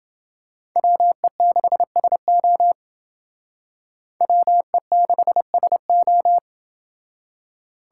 Morse code, heard as a keyed tone, about 28 wpm, 700 Hz. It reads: WE6SO WE6SO